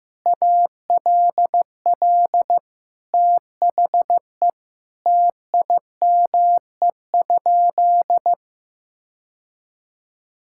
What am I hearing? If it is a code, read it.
Morse code, heard as a keyed tone, about 15 wpm, 700 Hz. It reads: ALL THE TIME?
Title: Sentences from Top 1000 Words - Review Part 5 - 15wpm